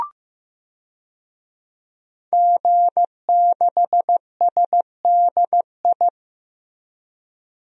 Morse code, heard as a keyed tone, about 15 wpm, 700 Hz. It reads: G6SDI
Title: DXCC-100 Call Signs - 5 Characters - Repeated 2x - 15wpm